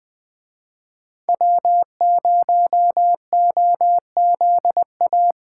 Morse code, heard as a keyed tone, about 20 wpm, 700 Hz. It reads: W0OZA
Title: Speed Racing - USA 1x3 Call Signs 17wpm